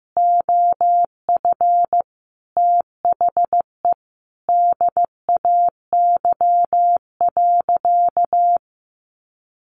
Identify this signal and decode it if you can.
Morse code, heard as a keyed tone, about 15 wpm, 700 Hz. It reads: OF THE DAY.
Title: Sentences from Top 1400 Words 15wpm